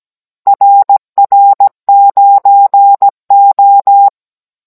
Morse code, heard as a keyed tone, about 17 wpm, 800 Hz. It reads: RR9O
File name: Cours-NR18-16mm